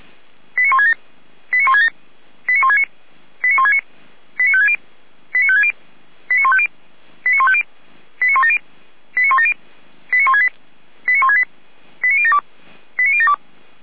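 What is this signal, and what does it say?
Five-tone selective call (ZVEI), heard as quick runs of five tones, tones of about 70 ms. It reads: 89167 89167 89169 89169 89560 89560 89150 89150 89170 89170 89168 89168 89072 89072
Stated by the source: ZVEI five-tone selective call 89220 received by SDR, from the Feuerwehr Orlamünde fire brigade audio server.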